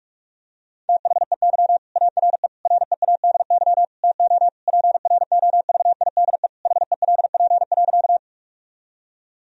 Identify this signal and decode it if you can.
Morse code, heard as a keyed tone, about 46 wpm, 700 Hz. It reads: THEY ARE READY TO PROVIDE HELP.
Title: Sentences from Top 1000 Words - Review Part 3 - 45wpm